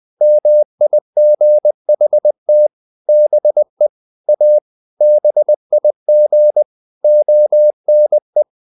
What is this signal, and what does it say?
Morse code, heard as a keyed tone, about 20 wpm, 600 Hz. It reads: MIGHT BE A BIG ONE
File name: the old man and the sea / the old man and the sea part 1 20wpm